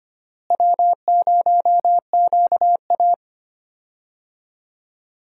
Morse code, heard as a keyed tone, about 25 wpm, 700 Hz. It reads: W0QA
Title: USA Call Signs - 4 Characters - Repeated 3x - 25wpm